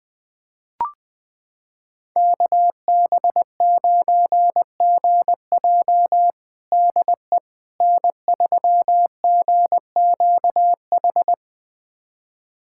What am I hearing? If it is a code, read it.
Morse code, heard as a keyed tone, about 20 wpm, 700 Hz. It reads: KB9GJ DE N3GQH
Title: USA Callsign DE Callsign - 5 Characters - 20wpm